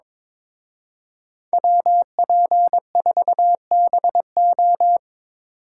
Morse code, heard as a keyed tone, about 26 wpm, 700 Hz. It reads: WP4BO
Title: Speed Racing - USA Call Signs 22wpm